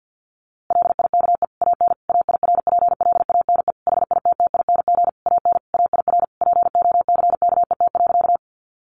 Morse code, heard as a keyed tone, about 50 wpm, 700 Hz. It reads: LIKE AN AIRPLANE HITTING AN AIR POCKET.
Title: A Sound Of Thunder 50wpm